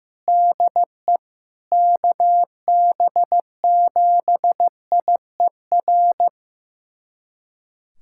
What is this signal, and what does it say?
Morse code, heard as a keyed tone, about 15 wpm, 700 Hz. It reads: DE KB7IER
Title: USA Callsign DE Callsign - 6 Characters - 15wpm